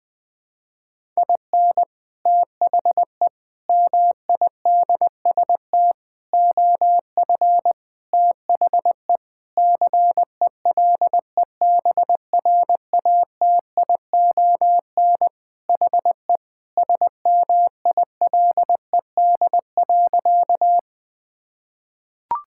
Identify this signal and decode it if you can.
Morse code, heard as a keyed tone, about 20 wpm, 700 Hz. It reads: IN THE MIDST OF THE CELEBRATION HE SMILED.
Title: Sentences with 5-Word Phrases - Spoken Beginning - 20wpm